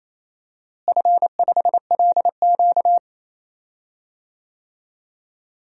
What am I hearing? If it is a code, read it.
Morse code, heard as a keyed tone, about 27 wpm, 700 Hz. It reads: F5LQ
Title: DXCC-100 Call Signs - 4 Characters - Repeated 3x - 28wpm